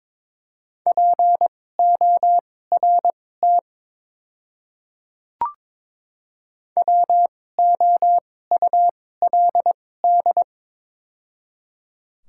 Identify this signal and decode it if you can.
Morse code, heard as a keyed tone, about 22 wpm, spaced as for 16 wpm, 700 Hz. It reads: PORT WOULD
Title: 2x Character Spacing - Top 200 Words - 22wpm